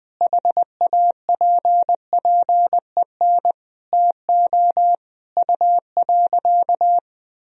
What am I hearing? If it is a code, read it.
Morse code, heard as a keyed tone, about 20 wpm, 700 Hz. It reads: HAPPEN TO U.
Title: Sentences from Top 500 Words and Abbreviations 20wpm